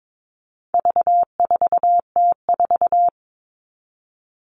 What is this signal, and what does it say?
Morse code, heard as a keyed tone, about 22 wpm, 700 Hz. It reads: V4T4
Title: All Cut Numbers - 4-Digit 22wpm